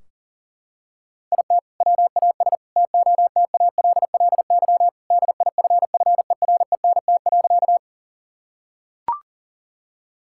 Morse code, heard as a keyed tone, about 37 wpm, 700 Hz. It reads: IT WAS TOTALLY DIFFERENT.
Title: Sentences from Top 1600 Words 40wpm